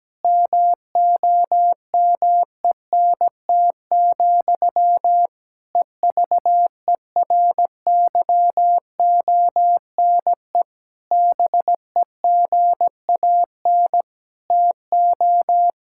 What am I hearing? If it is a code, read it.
Morse code, heard as a keyed tone, about 17 wpm, 700 Hz. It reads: MOMENT, EVERYONE BEGAN TO
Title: Sentences from Top 1000 Words - Review Part 5 - 17wpm